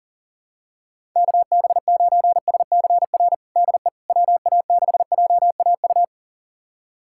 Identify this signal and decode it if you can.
Morse code, heard as a keyed tone, about 40 wpm, 700 Hz. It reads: KB9SCR DE WA6JAU
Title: USA Callsign DE Callsign - 6 Characters - 40wpm